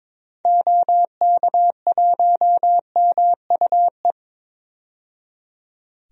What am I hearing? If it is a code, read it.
Morse code, heard as a keyed tone, about 22 wpm, 700 Hz. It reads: OK1MUE
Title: DXCC-100 Call Signs - 6 Characters - 22wpm